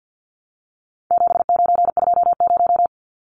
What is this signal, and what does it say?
Morse code, heard as a keyed tone, about 50 wpm, 700 Hz. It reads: 7920
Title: Numbers - 4-Digit 50wpm